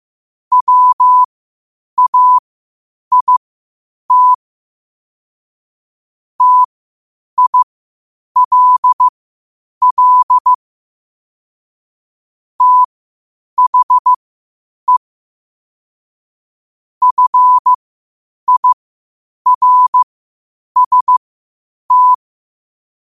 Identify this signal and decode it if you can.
Morse code, heard as a keyed tone, about 15 wpm, spaced as for 8 wpm, 1000 Hz. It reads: WAIT TILL THE FIRST
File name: qotdfiles / 8wpm